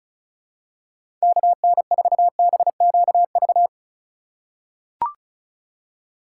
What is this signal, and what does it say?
Morse code, heard as a keyed tone, about 34 wpm, 700 Hz. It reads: KN4BQV
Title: USA Call Signs - 6 Characters - 2x Character Spacing - 35wpm